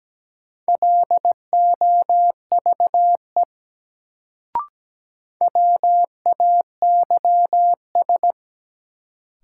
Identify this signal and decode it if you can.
Morse code, heard as a keyed tone, about 17 wpm, 700 Hz. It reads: LOVE WAYS
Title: Sets of 5 Words Encore - New Word Drill - 17wpm